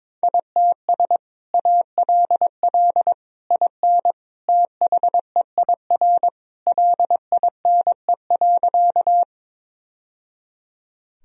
Morse code, heard as a keyed tone, about 21 wpm, 700 Hz. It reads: ITS ALL IN THEIR LINE.
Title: Sentences from Top 1000 Words - Review Part 1 - 22wpm